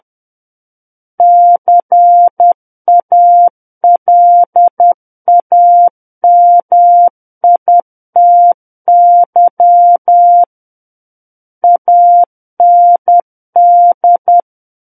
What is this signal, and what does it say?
Morse code, heard as a keyed tone, about 10 wpm, 700 Hz. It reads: CALAMITY AND
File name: Book One / 01-Chapter 1 - 10 WPM